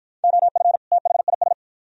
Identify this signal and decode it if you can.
Morse code, heard as a keyed tone, about 53 wpm, 700 Hz. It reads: OF THIS